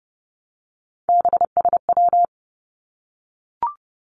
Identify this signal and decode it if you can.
Morse code, heard as a keyed tone, about 32 wpm, 700 Hz. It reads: BSW
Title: Three Challenging Letter-Number Combinations 30wpm